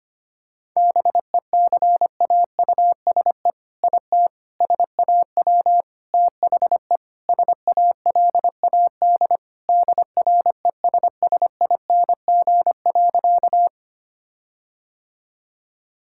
Morse code, heard as a keyed tone, about 25 wpm, 700 Hz. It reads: BECAUSE IT SAW THE SALAD DRESSING.
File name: practice / 25wpm / All-Age Appropriate Jokes - 25wpm